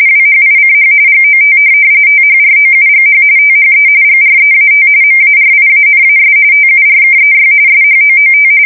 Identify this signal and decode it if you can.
RTTY, about 45.45 baud. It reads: WHICH GOT THROUGH THE GREAT MARIANAS TURKEY SHOOT
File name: fsk170 / USS-New-Jersey